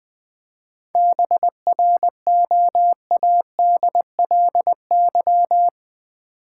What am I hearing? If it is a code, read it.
Morse code, heard as a keyed tone, about 20 wpm, 690 Hz. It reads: BROADLY
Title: 7 Letter Words - First 3 Letters Given - 20wpm